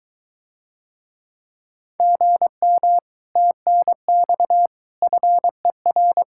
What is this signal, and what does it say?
Morse code, heard as a keyed tone, about 23 wpm, 700 Hz. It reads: GM TNX FER